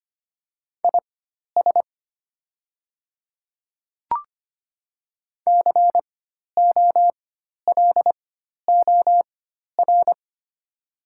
Morse code, heard as a keyed tone, about 25 wpm, spaced as for 12 wpm, 700 Hz. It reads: IS COLOR
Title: 4x Character Spacing - Top 300 Words - 25wpm